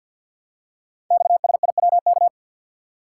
Morse code, heard as a keyed tone, about 48 wpm, 700 Hz. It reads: XSIWK
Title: Code Groups - 5 Letters - 50wpm